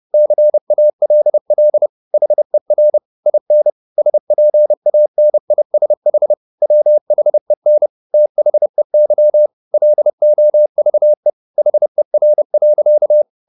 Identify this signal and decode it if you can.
Morse code, heard as a keyed tone, about 30 wpm, 600 Hz. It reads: CALL HER IN SPANISH WHEN THEY LOVE HER.
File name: the old man and the sea / the old man and the sea 1 30wpm